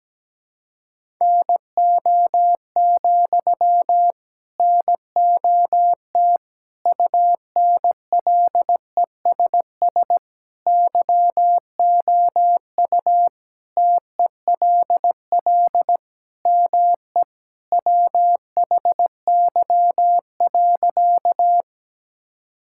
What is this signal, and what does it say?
Morse code, heard as a keyed tone, about 17 wpm, 700 Hz. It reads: NO, NOT UNLESS YOU TELL ME WHY.
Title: Sentences from Top 1000 Words - Review Part 6 - 17wpm